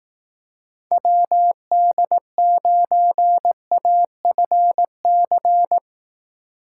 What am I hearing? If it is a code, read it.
Morse code, heard as a keyed tone, about 18 wpm, 700 Hz. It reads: WD9AFC
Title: Speed Racing - USA 2x3 Call Signs 15wpm